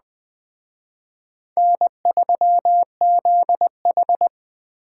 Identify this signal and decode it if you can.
Morse code, heard as a keyed tone, about 20 wpm, 700 Hz. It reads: N3ZH
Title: Speed Racing - USA Call Signs 17wpm